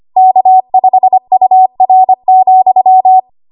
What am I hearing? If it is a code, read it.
Morse code, heard as a keyed tone, about 25 wpm, 750 Hz. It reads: K5UR,
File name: Archive / 25 WPM / 25WPM